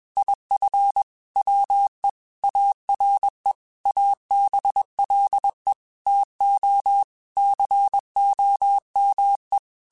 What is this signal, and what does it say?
Morse code, heard as a keyed tone, about 21 wpm, 800 Hz. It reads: IF WE ARE ABLE TO COME
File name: QSOs / QSOs-mp3s / WA7QOX 1@20